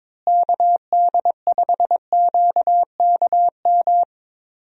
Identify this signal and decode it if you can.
Morse code, heard as a keyed tone, about 22 wpm, 700 Hz. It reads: KD5QKM
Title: USA Call Signs - 6 Characters - Repeated 3x - 22wpm